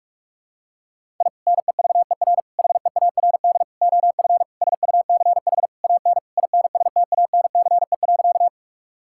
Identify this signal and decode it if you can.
Morse code, heard as a keyed tone, about 46 wpm, 700 Hz. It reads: I NEVER HEARD OF SUCH AN INSTANCE.
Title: Sentences from Top 1000 Words - Review Part 5 - 45wpm